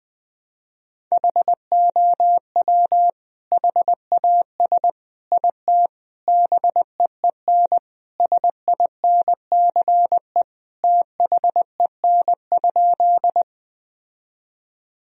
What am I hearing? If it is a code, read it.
Morse code, heard as a keyed tone, about 21 wpm, 700 Hz. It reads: HOW HAS IT BEEN SINCE THEN?
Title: Sentences from Top 1000 Words - Review Part 1 - 20wpm